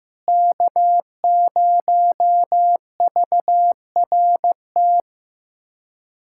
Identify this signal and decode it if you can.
Morse code, heard as a keyed tone, about 15 wpm, 700 Hz. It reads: K0VRT
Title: USA Callsign DE Callsign - 5 Characters - 15wpm